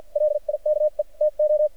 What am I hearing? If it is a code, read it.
Morse code, heard as a keyed tone, about 52 wpm, spaced as for 43 wpm, 600 Hz. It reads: LIKE TO
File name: archived / CRSnet